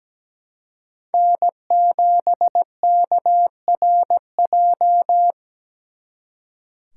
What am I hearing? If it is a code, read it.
Morse code, heard as a keyed tone, about 20 wpm, 700 Hz. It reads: N7KRJ
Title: Speed Racing - USA 1x3 Call Signs 17wpm